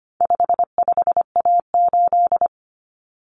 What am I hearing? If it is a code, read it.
Morse code, heard as a keyed tone, about 25 wpm, 700 Hz. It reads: 55A8
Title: Common Cut Numbers - 4-Digit 25wpm